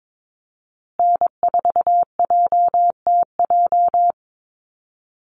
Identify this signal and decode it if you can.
Morse code, heard as a keyed tone, about 22 wpm, 700 Hz. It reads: N4JTJ
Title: USA Call Signs Repeated 3x 22wpm